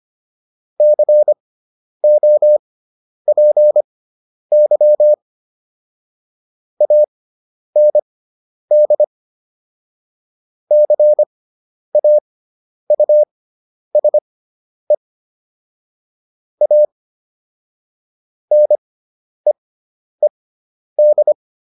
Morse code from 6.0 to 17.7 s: AND CAUSE A

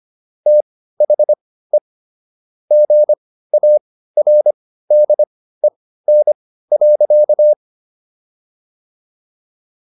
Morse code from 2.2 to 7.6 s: GARDEN.

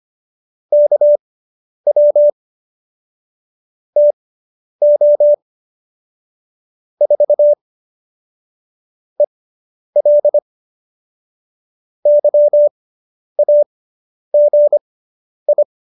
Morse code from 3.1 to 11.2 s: TO 4 EL

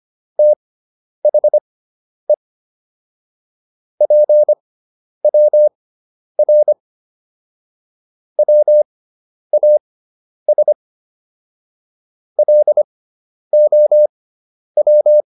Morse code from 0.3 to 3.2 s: THE

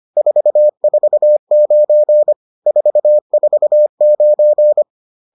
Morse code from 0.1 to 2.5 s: 449

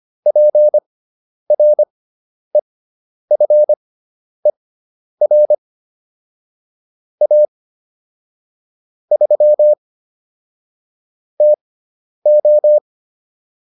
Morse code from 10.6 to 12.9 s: TO

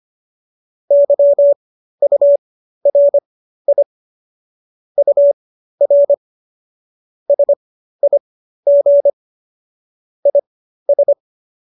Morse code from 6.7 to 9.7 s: SIG

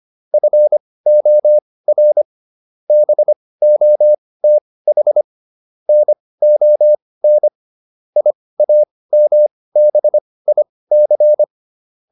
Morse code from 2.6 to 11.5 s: BOTH NON IAMBIC

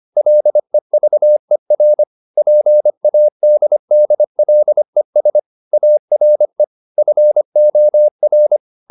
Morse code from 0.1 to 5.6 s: LEVER PADDLES

